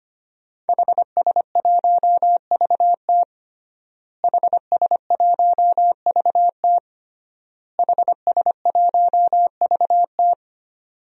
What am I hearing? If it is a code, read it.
Morse code, heard as a keyed tone, about 25 wpm, 700 Hz. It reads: HS1VT HS1VT HS1VT
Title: DXCC-100 Call Signs - 5 Characters - Repeated 3x - 25wpm